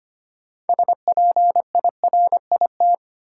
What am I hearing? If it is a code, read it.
Morse code, heard as a keyed tone, about 25 wpm, 700 Hz. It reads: SPIRIT
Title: Sentences from Top 1400 Words - New Word Drill - 25wpm